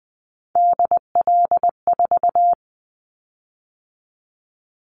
Morse code, heard as a keyed tone, about 19 wpm, 700 Hz. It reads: DL4
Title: Callsign Prefixes for Top 50 DXCC - 20wpm